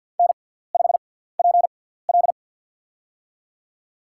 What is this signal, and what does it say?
Morse code, heard as a keyed tone, about 49 wpm, spaced as for 17 wpm, 690 Hz. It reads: N5PL
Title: USA Call Signs - 4 Characters - 6x Character Spacing - 50wpm